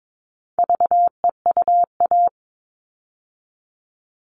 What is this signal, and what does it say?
Morse code, heard as a keyed tone, about 22 wpm, 700 Hz. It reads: VEUA
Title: All Cut Numbers - 4-Digit 22wpm